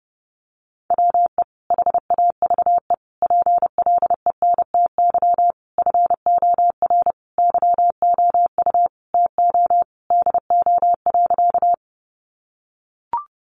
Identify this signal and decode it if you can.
Morse code, heard as a keyed tone, about 30 wpm, 700 Hz. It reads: WE HAVE PLENTY FOR YOU TO DO.
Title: Sentences from Top 1300 Words 30wpm